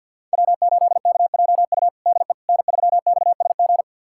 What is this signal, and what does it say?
Morse code, heard as a keyed tone, about 52 wpm, 700 Hz. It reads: W8KJU DE N3XSG
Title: USA Callsign DE Callsign - 5 Characters - 50wpm